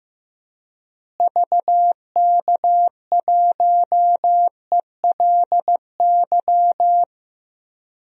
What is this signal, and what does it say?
Morse code, heard as a keyed tone, about 16 wpm, 700 Hz. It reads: VK1ELY
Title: DXCC-100 Call Signs - 6 Characters - Repeated 3x - 15wpm